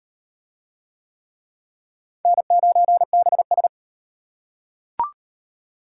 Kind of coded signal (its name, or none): Morse code